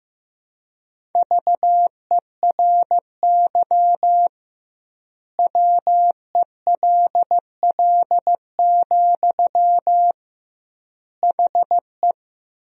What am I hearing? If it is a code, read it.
Morse code, heard as a keyed tone, about 15 wpm, 700 Hz. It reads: VERY WELL, HE
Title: 2x Word-Spacing - Sentences from Top 200 Words - 15wpm